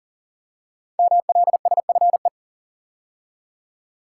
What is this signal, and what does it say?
Morse code, heard as a keyed tone, about 40 wpm, 700 Hz. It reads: MLSFE
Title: Code Groups - 5 Letters - 40wpm